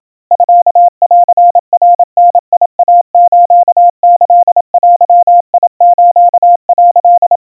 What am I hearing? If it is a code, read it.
Morse code (Japanese Wabun code), heard as a keyed tone, about 27 wpm, 700 Hz. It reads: ミンナダイスキデス」
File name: kawa / Morse sample / kamyu 100 cpm 700 Hz Crispy